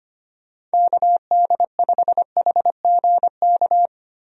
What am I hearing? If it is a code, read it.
Morse code, heard as a keyed tone, about 25 wpm, 700 Hz. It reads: KD5HGK
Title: USA Call Signs - 6 Characters - Repeated 3x - 25wpm